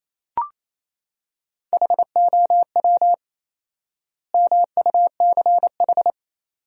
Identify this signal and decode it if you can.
Morse code, heard as a keyed tone, about 28 wpm, 700 Hz. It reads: HOW MUCH
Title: 4x Word-Spacing - Sentences from Top 200 Words - 28wpm